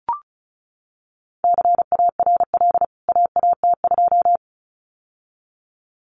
Morse code, heard as a keyed tone, about 34 wpm, 700 Hz. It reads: CARL AAT2
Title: CWT Exchange 35wpm